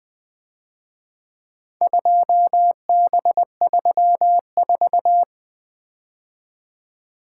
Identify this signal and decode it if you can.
Morse code, heard as a keyed tone, about 24 wpm, 700 Hz. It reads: 2B34
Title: Speed-Racing - Four Letter-Number Combinations - 20wpm